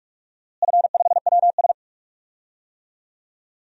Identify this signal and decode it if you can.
Morse code, heard as a keyed tone, about 41 wpm, 700 Hz. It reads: F5WS
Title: DXCC-100 Call Signs - 4 Characters - Repeated 3x - 45wpm